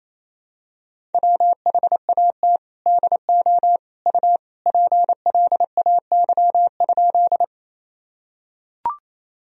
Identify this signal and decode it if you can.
Morse code, heard as a keyed tone, about 28 wpm, 700 Hz. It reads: WHAT DO U PLAY?
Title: Sentences from Top 200 Words and Abbreviations 28wpm